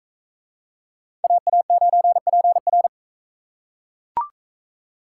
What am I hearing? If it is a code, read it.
Morse code, heard as a keyed tone, about 38 wpm, 700 Hz. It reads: AA9PR